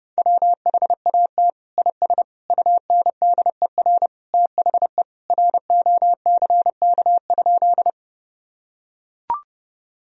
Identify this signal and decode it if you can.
Morse code, heard as a keyed tone, about 30 wpm, 700 Hz. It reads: WHAT IS UNDER THE ROCK?
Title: Sentences from Top 1000 Words - Review Part 2 - 30wpm